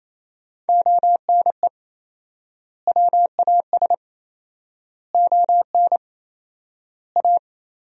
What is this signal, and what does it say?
Morse code, heard as a keyed tone, about 28 wpm, 700 Hz. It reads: ONE WAS ON A
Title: 4x Word-Spacing - Sentences from Top 300 Words - 28wpm